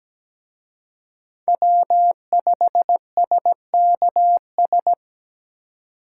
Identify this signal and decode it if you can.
Morse code, heard as a keyed tone, about 20 wpm, 700 Hz. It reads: W5SKS
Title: Speed Racing - USA Call Signs 17wpm